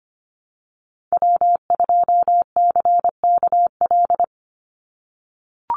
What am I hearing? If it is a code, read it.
Morse code, heard as a keyed tone, about 25 wpm, 690 Hz. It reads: W2CKL
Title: NA SOTA Activator Callsigns 25wpm